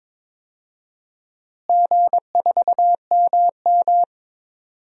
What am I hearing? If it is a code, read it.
Morse code, heard as a keyed tone, about 22 wpm, 700 Hz. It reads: G4MM
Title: DXCC-100 Call Signs - 4 Characters - Repeated 2x - 22wpm